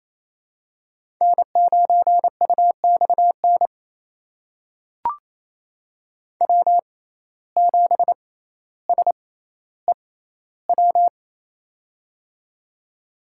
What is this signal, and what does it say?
Morse code, heard as a keyed tone, about 28 wpm, spaced as for 14 wpm, 700 Hz. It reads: N9UXN W7SEW
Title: USA Call Signs - 5 Characters - 6x Character Spacing - 28wpm